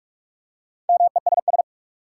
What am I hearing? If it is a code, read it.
Morse code, heard as a keyed tone, about 46 wpm, 700 Hz. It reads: MESS
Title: Four Letter Words - 45wpm